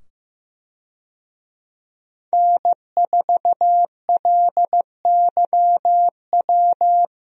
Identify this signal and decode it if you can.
Morse code, heard as a keyed tone, about 15 wpm, 700 Hz. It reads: N4LYW
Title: USA Call Signs - 5 Characters - 15wpm